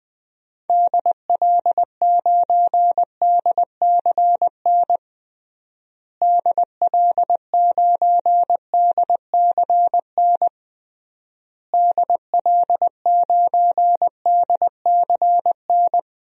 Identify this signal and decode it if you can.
Morse code, heard as a keyed tone, about 20 wpm, 700 Hz. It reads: DL9DCN DL9DCN DL9DCN
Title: DXCC-100 Call Signs - 6 Characters - Repeated 3x - 20wpm